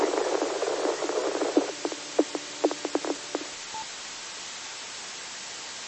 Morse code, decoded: none

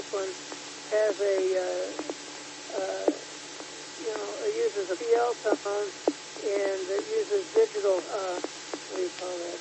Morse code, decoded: none

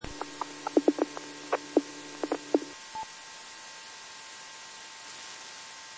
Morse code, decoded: none

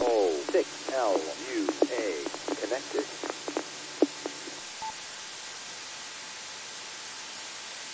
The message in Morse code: none

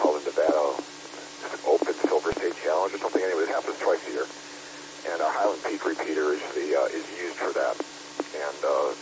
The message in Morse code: none

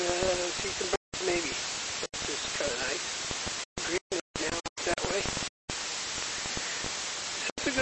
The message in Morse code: none

